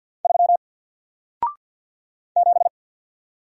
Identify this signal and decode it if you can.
Morse code, heard as a keyed tone, about 48 wpm, 700 Hz. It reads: 3 7
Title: Single Letter-Number Alt 50wpm